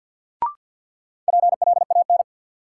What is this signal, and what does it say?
Morse code, heard as a keyed tone, about 53 wpm, 700 Hz. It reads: PLAN
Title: Sentences from Top 500 Words - New Word Drill - 50wpm